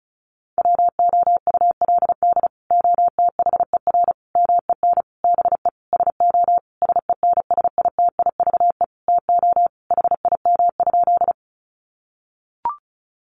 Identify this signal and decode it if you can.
Morse code, heard as a keyed tone, about 34 wpm, 700 Hz. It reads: WOULD OTHER MEN BE SO SENSITIVE TO HIM?
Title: Sentences from Top 1400 Words 35wpm